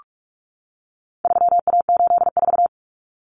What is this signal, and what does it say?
Morse code, heard as a keyed tone, about 44 wpm, 700 Hz. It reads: 3A84
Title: Common Cut Numbers - 4-Digit 45wpm